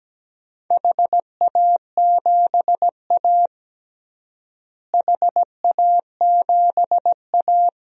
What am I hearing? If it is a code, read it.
Morse code, heard as a keyed tone, about 17 wpm, 700 Hz. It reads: HA7A HA7A
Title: DXCC-100 Call Signs - 4 Characters - Repeated 3x - 17wpm